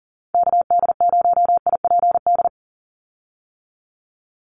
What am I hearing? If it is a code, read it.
Morse code, heard as a keyed tone, about 40 wpm, 700 Hz. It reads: KD0IPD
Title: USA Call Signs 40wpm